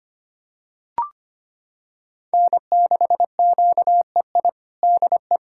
Morse code, heard as a keyed tone, about 25 wpm, 700 Hz. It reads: N6QEI DE
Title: USA Callsign DE Callsign - 5 Characters - 25wpm